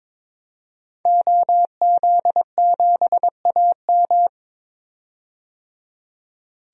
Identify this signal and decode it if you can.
Morse code, heard as a keyed tone, about 22 wpm, 700 Hz. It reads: OZ7AM
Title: DXCC-100 Call Signs - 5 Characters - 22wpm